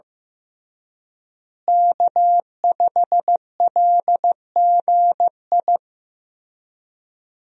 Morse code, heard as a keyed tone, about 15 wpm, 700 Hz. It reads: K5LGI
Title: USA Call Signs - 5 Characters - Repeated 3x - 15wpm